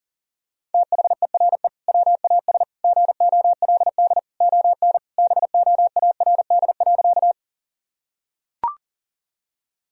Morse code, heard as a keyed tone, about 40 wpm, 700 Hz. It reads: THERE WAS GOLD ON BOARD.
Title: Sentences from Top 1000 Words - Review Part 4 - 40wpm